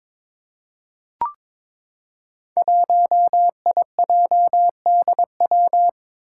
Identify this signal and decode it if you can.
Morse code, heard as a keyed tone, about 22 wpm, 700 Hz. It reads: 1IJDW
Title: Five Challenging Letter-Number Combinations 22wpm